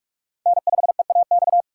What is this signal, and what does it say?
Morse code, heard as a keyed tone, about 44 wpm, 700 Hz. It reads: N5EAX